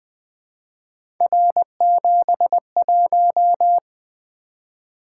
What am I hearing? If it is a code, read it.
Morse code, heard as a keyed tone, about 20 wpm, 700 Hz. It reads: R71
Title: Three Letter-Number Combinations - 20wpm